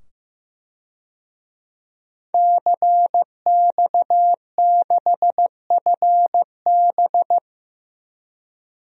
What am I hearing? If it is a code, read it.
Morse code, heard as a keyed tone, about 15 wpm, 700 Hz. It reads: CX6FB
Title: DXCC-100 Call Signs - 5 Characters - 15wpm